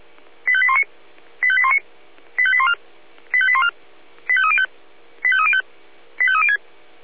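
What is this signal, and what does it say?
Five-tone selective call (ZVEI), heard as quick runs of five tones, tones of about 70 ms. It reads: 85819 85819 85814 85814 85305 85305 85306
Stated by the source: ZVEI five-tone selective call received on an SDR.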